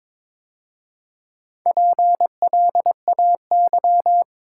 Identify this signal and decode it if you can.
Morse code, heard as a keyed tone, about 22 wpm, 690 Hz. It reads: PLAY
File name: practice / 22wpm / Easy Sentences with Hints - 22wpm